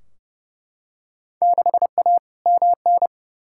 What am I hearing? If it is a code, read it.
Morse code, heard as a keyed tone, about 30 wpm, 700 Hz. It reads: 6A MN
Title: Field Day Exchanges Encore - 30wpm